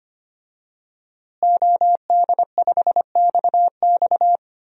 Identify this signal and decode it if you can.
Morse code, heard as a keyed tone, about 25 wpm, 700 Hz. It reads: OD5XX
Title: DXCC-100 Call Signs - 5 Characters - Repeated 2x - 25wpm